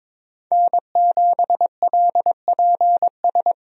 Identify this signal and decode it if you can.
Morse code, heard as a keyed tone, about 23 wpm, 700 Hz. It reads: N7LPS
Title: USA Call Signs - 5 Characters - Repeated 3x - 22wpm